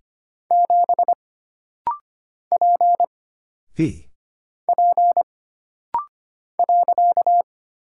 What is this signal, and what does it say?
Morse code, heard as a keyed tone, about 25 wpm, 700 Hz. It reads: 7 P P .
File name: Single Character (Letter:Number:Punctuation:Prosign) - Rapid-Fire - 25wpm